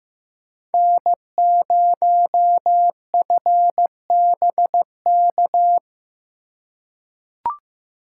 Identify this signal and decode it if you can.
Morse code, heard as a keyed tone, about 15 wpm, 700 Hz. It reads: N0FBK
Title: USA Call Signs - 5 Characters - 3x Character Spacing - 15wpm